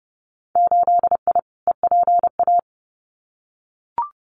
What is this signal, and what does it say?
Morse code, heard as a keyed tone, about 30 wpm, 700 Hz. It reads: 8I EPA
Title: Winter Field Day Exchanges Encore 30wpm